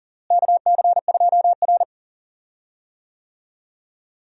Morse code, heard as a keyed tone, about 40 wpm, 700 Hz. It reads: KC2R
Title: USA Call Signs - 4 Characters - Repeated 2x - 40wpm